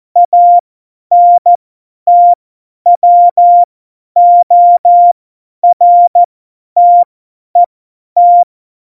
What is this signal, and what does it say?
Morse code, heard as a keyed tone, about 14 wpm, spaced as for 10 wpm, 700 Hz. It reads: ANTWORTET
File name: Morsekurs / Witze / Witz 21. Januar 2020, 50 F70